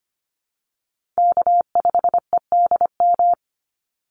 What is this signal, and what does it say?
Morse code, heard as a keyed tone, about 25 wpm, 700 Hz. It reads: K5EDM
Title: Call Signs Repeated 3x - 25wpm